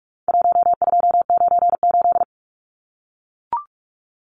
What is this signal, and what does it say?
Morse code, heard as a keyed tone, about 44 wpm, 700 Hz. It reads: 1298
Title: Numbers - 4-Digit 45wpm